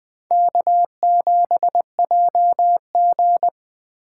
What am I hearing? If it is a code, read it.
Morse code, heard as a keyed tone, about 21 wpm, 700 Hz. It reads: K7JG